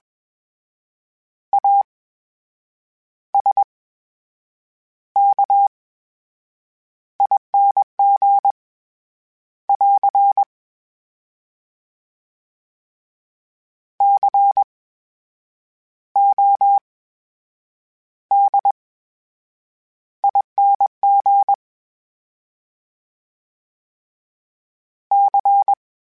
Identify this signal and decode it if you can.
Morse code, heard as a keyed tone, about 21 wpm, spaced as for 8 wpm, 800 Hz. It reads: ASKING <AR> CODING C
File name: Special INGs-mp3s / Special INGs-mp3s / VerySlowToFaster Ing @20